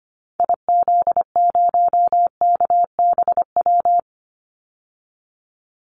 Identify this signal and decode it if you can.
Morse code, heard as a keyed tone, about 25 wpm, 700 Hz. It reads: IZ0KBW
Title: Call Signs Repeated 2x - 25wpm